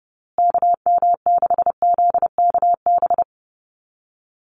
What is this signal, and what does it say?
Morse code, heard as a keyed tone, about 29 wpm, 700 Hz. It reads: KM6ZKB
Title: USA Call Signs Repeated 2x 30wpm